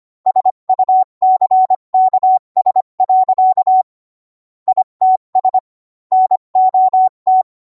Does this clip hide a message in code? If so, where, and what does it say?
Yes: Morse code from 0.3 to 7.4 s, heard as a keyed tone, about 25 wpm, 750 Hz. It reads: SUCKS. ITS NOT